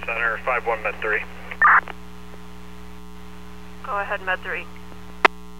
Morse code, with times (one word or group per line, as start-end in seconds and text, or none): none